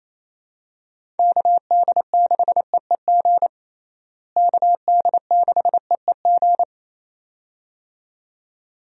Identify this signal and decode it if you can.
Morse code, heard as a keyed tone, about 27 wpm, 700 Hz. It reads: KD6EEG KD6EEG
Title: USA Call Signs - 6 Characters - Repeated 2x - 28wpm